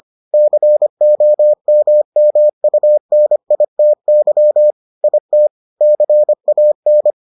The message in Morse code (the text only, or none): COMMUNITY IT CAN